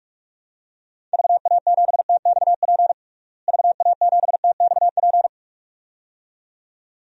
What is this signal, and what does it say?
Morse code, heard as a keyed tone, about 42 wpm, 700 Hz. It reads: VA7TXP VA7TXP
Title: DXCC-100 Call Signs - 6 Characters - Repeated 2x - 45wpm